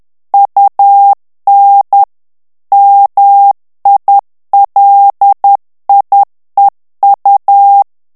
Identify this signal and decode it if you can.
Morse code, heard as a keyed tone, about 10 wpm, 800 Hz. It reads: UN MILIEU
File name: FRANCAIS TEXTE CLAIR 2 VITESSE 10 MOTS MINUTES